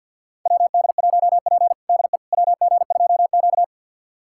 Morse code, heard as a keyed tone, about 52 wpm, 700 Hz. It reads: WD1P DE WG2Q
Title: USA Callsign DE Callsign - 4 Characters - 50wpm